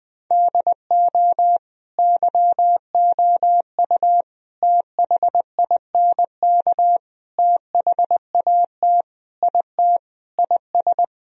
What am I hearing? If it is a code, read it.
Morse code, heard as a keyed tone, about 20 wpm, 700 Hz. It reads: DO YOU THINK THAT IT IS
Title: Sentences from Top 1000 Words - Review Part 4 - 20wpm